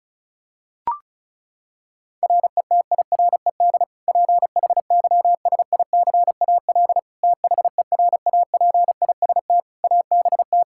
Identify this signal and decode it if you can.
Morse code, heard as a keyed tone, about 36 wpm, 700 Hz. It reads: RETIRED PHYSICAL THERAPIST ABT
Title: Practice QSO - Exchange 3 - More Info - 35wpm